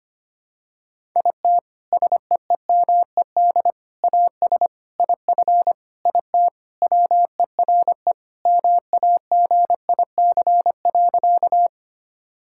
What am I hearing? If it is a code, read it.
Morse code, heard as a keyed tone, about 25 wpm, 700 Hz. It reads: IT SEEMED AS IF IT WERE MAGIC.
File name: practice / 25wpm / Sentences with 4-Word Phrases - 25wpm